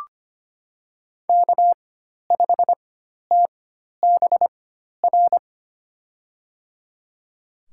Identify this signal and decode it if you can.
Morse code, heard as a keyed tone, about 25 wpm, spaced as for 12 wpm, 700 Hz. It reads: K5TBR
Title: USA Call Signs - 5 Characters - 4x Character Spacing - 25wpm